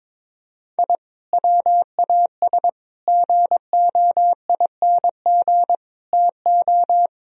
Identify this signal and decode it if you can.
Morse code, heard as a keyed tone, about 22 wpm, 700 Hz. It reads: I WAS GOING TO
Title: Speed-Racing - Sets of 5 Words 22wpm